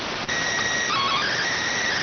SSTV (slow-scan television), PD 290.